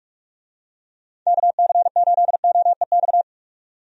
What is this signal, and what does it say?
Morse code, heard as a keyed tone, about 42 wpm, 700 Hz. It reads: KC8OEX
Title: USA Call Signs - 6 Characters - 45wpm